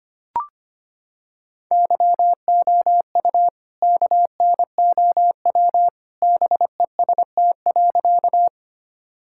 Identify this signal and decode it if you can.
Morse code, heard as a keyed tone, about 25 wpm, 700 Hz. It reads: YOU KNOW BEST.
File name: practice / 25wpm / Sentences with 2-Word Phrases - 25wpm